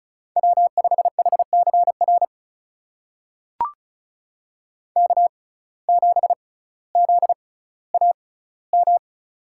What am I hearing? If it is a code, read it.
Morse code, heard as a keyed tone, about 36 wpm, spaced as for 18 wpm, 700 Hz. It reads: W5HCR K7ZAM
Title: USA Call Signs - 5 Characters - 6x Character Spacing - 35wpm